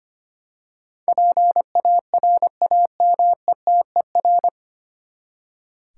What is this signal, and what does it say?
Morse code, heard as a keyed tone, about 25 wpm, 700 Hz. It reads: PARAMETER
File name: Nine Letter Words - 25wpm